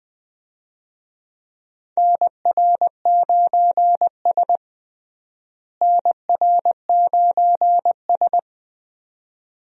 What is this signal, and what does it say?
Morse code, heard as a keyed tone, about 20 wpm, 700 Hz. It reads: NR9S NR9S
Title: USA Call Signs - 4 Characters - Repeated 2x - 20wpm